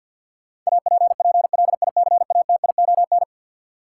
Morse code, heard as a keyed tone, about 51 wpm, 700 Hz. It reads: APPLICATION
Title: Sentences from Top 1200 Words - New Word Drill - 50wpm